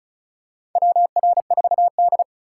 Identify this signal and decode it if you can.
Morse code, heard as a keyed tone, about 36 wpm, 700 Hz. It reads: WR4D
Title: USA Call Signs - 4 Characters - 2x Character Spacing - 35wpm